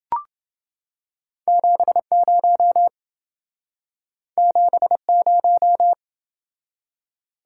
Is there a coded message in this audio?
Yes: Morse code from 1.5 to 5.9 s, heard as a keyed tone, about 28 wpm, 700 Hz. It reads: 70 70